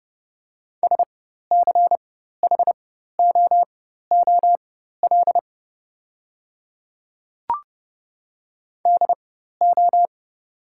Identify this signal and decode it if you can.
Morse code, heard as a keyed tone, about 30 wpm, spaced as for 14 wpm, 700 Hz. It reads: SCHOOL DO
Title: 4x Character Spacing - Top 300 Words - 30wpm